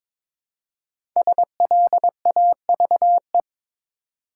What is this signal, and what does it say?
Morse code, heard as a keyed tone, about 22 wpm, 700 Hz. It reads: SLAVE
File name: 5 Letter Words - First 3 Letters Given - 22wpm